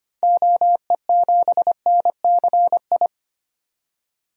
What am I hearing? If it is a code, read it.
Morse code, heard as a keyed tone, about 25 wpm, 700 Hz. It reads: OE7NCI
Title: DXCC-100 Call Signs - 6 Characters - 25wpm